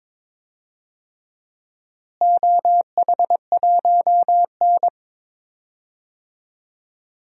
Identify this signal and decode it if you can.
Morse code, heard as a keyed tone, about 22 wpm, 700 Hz. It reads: OH1N
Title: DXCC-100 Call Signs - 4 Characters - Repeated 3x - 22wpm